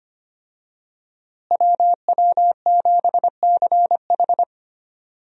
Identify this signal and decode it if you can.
Morse code, heard as a keyed tone, about 25 wpm, 700 Hz. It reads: WW7CH
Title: USA Call Signs - 5 Characters - Repeated 2x - 25wpm